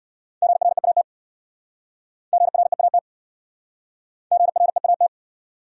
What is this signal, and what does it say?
Morse code, heard as a keyed tone, about 67 wpm, 700 Hz. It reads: GREAT GREAT GREAT